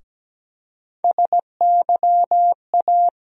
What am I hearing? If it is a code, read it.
Morse code, heard as a keyed tone, about 17 wpm, 700 Hz. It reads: SYA
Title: USA Call Sign Postfixes - 3 Letters 17wpm